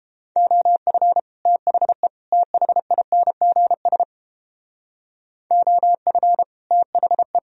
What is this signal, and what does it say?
Morse code, heard as a keyed tone, about 31 wpm, 700 Hz. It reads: OF THE THINGS OF THE